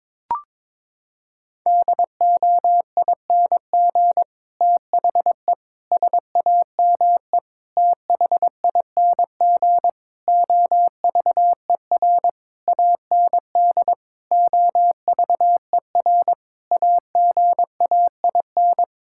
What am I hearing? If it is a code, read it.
Morse code, heard as a keyed tone, about 22 wpm, 700 Hz. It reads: DOING THE SAME THING OVER AND OVER AGAIN
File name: practice / 22wpm / Sets of 8 Words - 22wpm